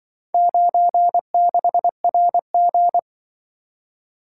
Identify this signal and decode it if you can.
Morse code, heard as a keyed tone, about 24 wpm, 700 Hz. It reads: 96RG
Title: Speed-Racing - Four Letter-Number Combinations - 20wpm